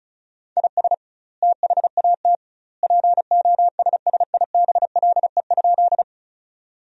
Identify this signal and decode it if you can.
Morse code, heard as a keyed tone, about 36 wpm, 700 Hz. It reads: IS THAT POSSIBLE?
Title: 2x Word-Spacing - Sentences from Top 500 Words - 35wpm